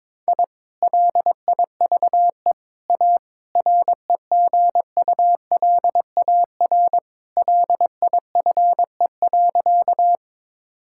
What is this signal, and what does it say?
Morse code, heard as a keyed tone, about 22 wpm, 700 Hz. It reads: I LIVE A REGULAR LIFE.
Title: Sentences from Top 1000 Words - Review Part 6 - 22wpm